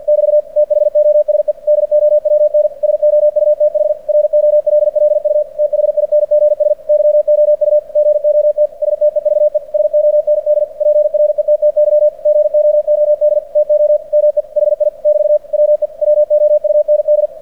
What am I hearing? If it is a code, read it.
Morse code, heard as a keyed tone, about 46 wpm, 600 Hz. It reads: <BT> THOSE DOWN SOUTH ROLLU THINGS YOU GOT STEVE SOND PRETTY GOOD TO ME HI <BT> WE WOUND
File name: archived / CRSnet